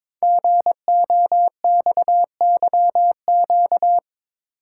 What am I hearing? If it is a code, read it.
Morse code, heard as a keyed tone, about 22 wpm, 700 Hz. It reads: GOXYQ
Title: Code Groups - 5 Letter:Numbers - 22wpm